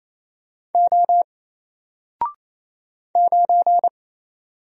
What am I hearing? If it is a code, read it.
Morse code, heard as a keyed tone, about 27 wpm, 700 Hz. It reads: O 9
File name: Single-Letter-Number-28wpm